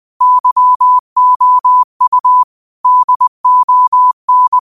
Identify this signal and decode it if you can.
Morse code, heard as a keyed tone, about 20 wpm, 1000 Hz. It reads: YOU DON